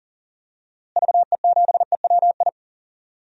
Morse code, heard as a keyed tone, about 37 wpm, 700 Hz. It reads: VE7EWI
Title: DXCC-100 Call Signs - 6 Characters - Repeated 2x - 40wpm